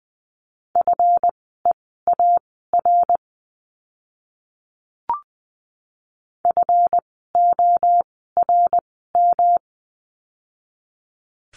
Morse code, heard as a keyed tone, about 20 wpm, spaced as for 14 wpm, 700 Hz. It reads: FEAR FORM
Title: 139 Lesson 23f - Letter f - Words 20wpm